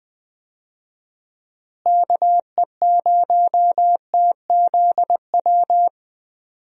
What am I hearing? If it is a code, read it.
Morse code, heard as a keyed tone, about 20 wpm, 700 Hz. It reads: KE0TZW
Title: USA Call Signs - 6 Characters - Repeated 3x - 20wpm